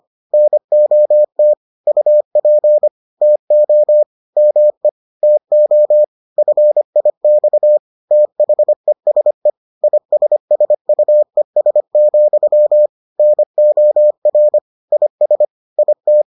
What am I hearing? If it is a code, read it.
Morse code, heard as a keyed tone, about 25 wpm, 600 Hz. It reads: NOT UP TO ME TO FIX THESE ISSUES, NOR IS IT